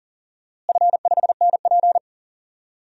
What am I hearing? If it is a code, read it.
Morse code, heard as a keyed tone, about 38 wpm, 700 Hz. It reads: F5NP
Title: DXCC-100 Call Signs - 4 Characters - Repeated 3x - 40wpm